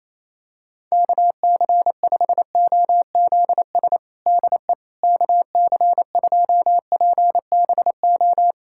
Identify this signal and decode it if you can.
Morse code, heard as a keyed tone, about 28 wpm, 700 Hz. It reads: KC5OZS DE KC2PBO
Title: USA Callsign DE Callsign 28wpm